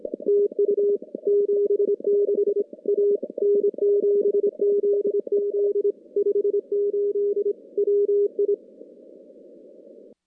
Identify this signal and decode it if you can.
Morse code, heard as a keyed tone, about 25 wpm, 410 Hz. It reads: TU 76 A N7ZZ 58 WI
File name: contests / CW-SS-k5tr- -2300